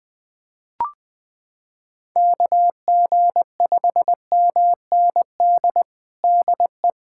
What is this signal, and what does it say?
Morse code, heard as a keyed tone, about 20 wpm, 700 Hz. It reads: KG5MND DE
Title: USA Callsign DE Callsign 20wpm